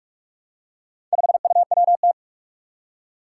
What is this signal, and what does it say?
Morse code, heard as a keyed tone, about 46 wpm, 700 Hz. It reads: 5UWT
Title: Four Challenging Letter-Number Combinations 45wpm